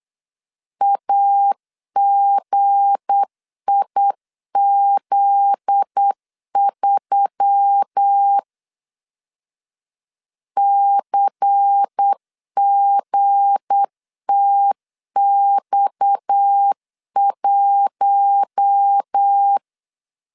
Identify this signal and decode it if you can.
Morse code, heard as a keyed tone, about 8 wpm, 800 Hz. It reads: AGIZ3 CGTX1
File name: LEZIONI / 7 GF9Z / 7 GF9Z 8wpm 3213